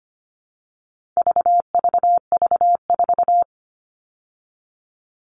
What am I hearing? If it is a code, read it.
Morse code, heard as a keyed tone, about 25 wpm, 700 Hz. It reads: VVV4
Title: All Cut Numbers - 4-Digit 25wpm